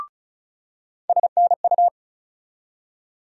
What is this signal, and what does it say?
Morse code, heard as a keyed tone, about 36 wpm, 700 Hz. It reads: SNU
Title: USA Call Sign Postfixes - 3 Letters 35wpm